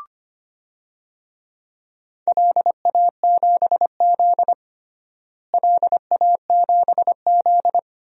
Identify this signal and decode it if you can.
Morse code, heard as a keyed tone, about 25 wpm, 700 Hz. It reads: LA7Z LA7Z
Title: DXCC-100 Call Signs - 4 Characters - Repeated 3x - 25wpm